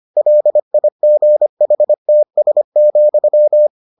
Morse code, heard as a keyed tone, about 25 wpm, 600 Hz. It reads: LIGHTS,